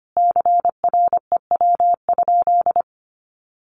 Morse code, heard as a keyed tone, about 25 wpm, 700 Hz. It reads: CREW?
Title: Sentences from Top 1500 Words 25wpm